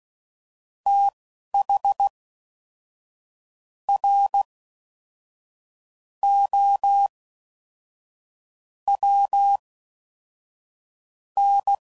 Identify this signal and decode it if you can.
Morse code, heard as a keyed tone, about 16 wpm, spaced as for 11 wpm, 800 Hz. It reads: TH R O W N
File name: Prefixes / FLs & THs / Fast TH Prefixes